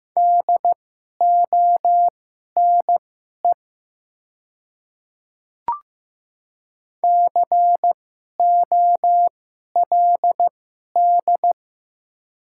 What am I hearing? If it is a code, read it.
Morse code, heard as a keyed tone, about 15 wpm, spaced as for 11 wpm, 690 Hz. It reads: DONE COLD
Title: 2x Character Spacing - Top 500 Words - 15wpm